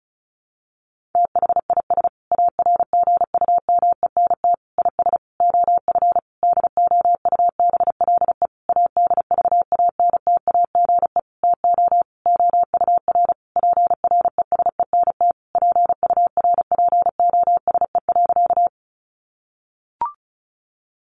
Morse code, heard as a keyed tone, about 34 wpm, 700 Hz. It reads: THIS ARGUMENT IS OF DOUBLE ADVANTAGE TO OUR PRESENT PURPOSE.
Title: Sentences from Top 1400 Words 35wpm